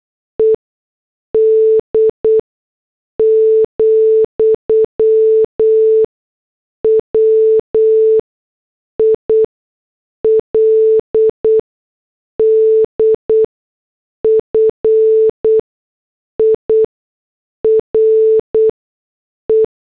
Morse code, heard as a keyed tone, about 8 wpm, spaced as for 6 wpm, 440 Hz. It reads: ED, WILDFIRE